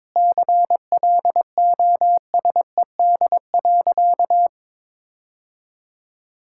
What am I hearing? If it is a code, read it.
Morse code, heard as a keyed tone, about 22 wpm, 700 Hz. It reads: CLOSED.
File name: practice / 22wpm / Sentences with 3-Word Phrases - 22wpm